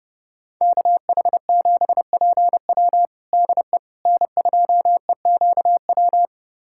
Morse code, heard as a keyed tone, about 30 wpm, 700 Hz. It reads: KH7PW DE N2EQW
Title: USA Callsign DE Callsign - 5 Characters - 30wpm